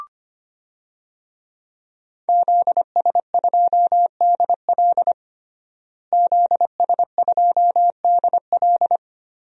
Morse code, heard as a keyed tone, about 25 wpm, 700 Hz. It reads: ZS2DL ZS2DL
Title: DXCC-100 Call Signs - 5 Characters - Repeated 2x - 25wpm